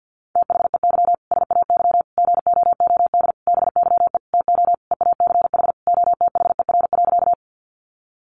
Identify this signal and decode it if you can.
Morse code, heard as a keyed tone, about 50 wpm, 700 Hz. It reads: THEY SAY GOOD BYE TO EACH OTHER.
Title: Easy Phrases 50wpm